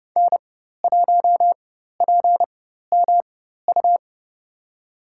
Morse code, heard as a keyed tone, about 30 wpm, spaced as for 14 wpm, 700 Hz. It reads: N1PMU
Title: USA Call Signs - 5 Characters - 4x Character Spacing - 30wpm